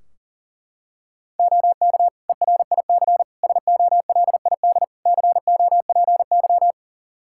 Morse code, heard as a keyed tone, about 40 wpm, 700 Hz. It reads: OK ERIC SOLID COPY
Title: Practice QSO - Exchange 3 - More Info - 40wpm